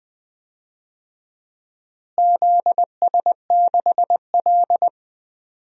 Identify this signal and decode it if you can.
Morse code, heard as a keyed tone, about 20 wpm, 700 Hz. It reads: ZS6L
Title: DXCC-100 Call Signs - 4 Characters - Repeated 2x - 20wpm